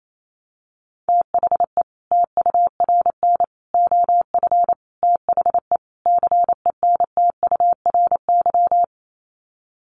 Morse code, heard as a keyed tone, about 28 wpm, 700 Hz. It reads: THE TURN OF THE CENTURY
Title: Sets of 5 Words 28wpm